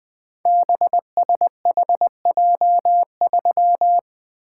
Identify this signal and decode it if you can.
Morse code, heard as a keyed tone, about 20 wpm, 700 Hz. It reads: BSHJ3
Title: Code Groups - 5 Letter:Numbers - 20wpm